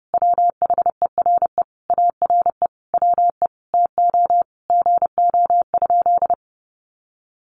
Morse code, heard as a keyed tone, about 29 wpm, 700 Hz. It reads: WHERE ARE WE TO GO?
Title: Sentences from Top 200 Words 30wpm